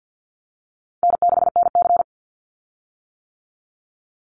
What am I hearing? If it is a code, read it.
Morse code, heard as a keyed tone, about 50 wpm, 700 Hz. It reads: N6NC
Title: Easy Call Signs - 50wpm